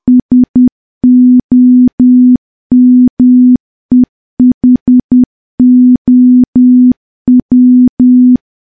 Morse code, heard as a keyed tone, about 10 wpm, 250 Hz. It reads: SOMEHOW
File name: morse / morse-training-10WPM